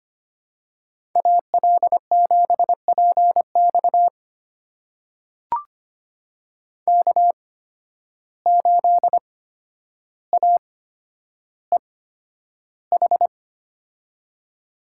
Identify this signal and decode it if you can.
Morse code, heard as a keyed tone, about 25 wpm, spaced as for 11 wpm, 700 Hz. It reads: AL7PX K8AEH
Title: USA Call Signs - 8x Character Spacing - 25wpm